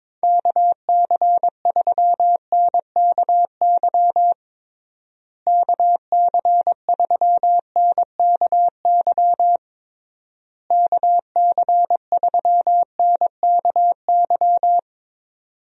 Morse code, heard as a keyed tone, about 22 wpm, 700 Hz. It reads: KC3NKY KC3NKY KC3NKY
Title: USA Call Signs - 6 Characters - Repeated 3x - 22wpm